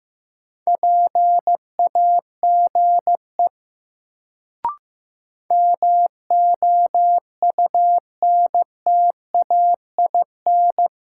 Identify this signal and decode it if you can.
Morse code, heard as a keyed tone, about 15 wpm, 700 Hz. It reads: PAGE MOUNTAIN
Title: Sentences from Top 1000 Words - Review Part 1 - New Word Drill - 15wpm